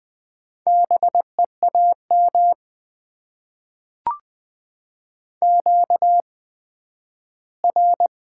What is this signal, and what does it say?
Morse code, heard as a keyed tone, about 20 wpm, spaced as for 12 wpm, 700 Hz. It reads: BEAM QR
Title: Top 90 QSO Elements - 8x Character Spacing - 20wpm